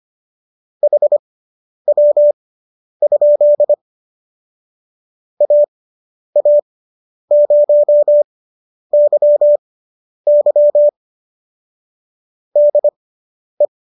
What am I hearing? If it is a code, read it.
Morse code, heard as a keyed tone, about 25 wpm, spaced as for 10 wpm, 590 Hz. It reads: HW? AA0YY DE